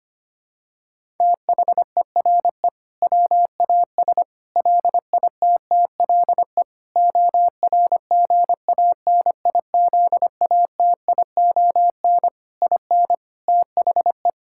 Morse code, heard as a keyed tone, about 25 wpm, 700 Hz. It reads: THERE WAS LITTLE ORGANIZATION IN THE
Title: Sentences from Top 1000 Words - Review Part 5 - 25wpm